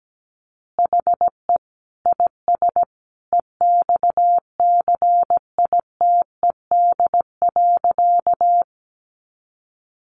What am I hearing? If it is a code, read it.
Morse code, heard as a keyed tone, about 17 wpm, 700 Hz. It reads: HE IS EXCITED.
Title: Additional Easy Phrases 17wpm